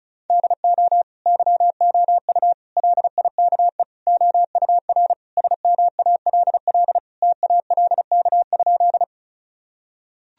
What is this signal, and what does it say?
Morse code, heard as a keyed tone, about 34 wpm, 700 Hz. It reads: DO YOU LIKE OUR SMALL TALK?
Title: Sentences from Top 1000 Words - Review Part 1 - 35wpm